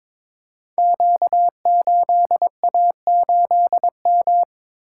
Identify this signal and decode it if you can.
Morse code, heard as a keyed tone, about 22 wpm, 690 Hz. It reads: Q8A8M
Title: Code Groups - 5 Letter:Numbers - 22wpm